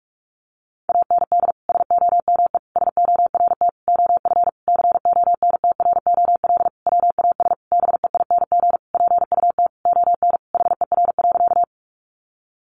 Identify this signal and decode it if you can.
Morse code, heard as a keyed tone, about 46 wpm, 700 Hz. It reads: AND SOME SORT OF CONTROL WAS BEING PUT ON HER.
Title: Sentences from Top 1300 Words 45wpm